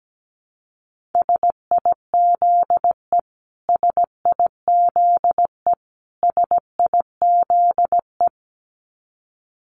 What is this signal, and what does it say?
Morse code, heard as a keyed tone, about 17 wpm, 700 Hz. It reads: SIZE SIZE SIZE
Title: Top 500 Words Repeated 17wpm